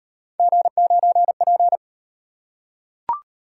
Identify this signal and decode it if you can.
Morse code, heard as a keyed tone, about 38 wpm, 700 Hz. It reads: G9P